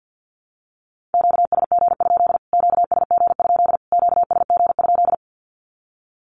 Morse code, heard as a keyed tone, about 50 wpm, 700 Hz. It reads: QSG? QSG? QSG?
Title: Q-Codes Repeated 3x 50wpm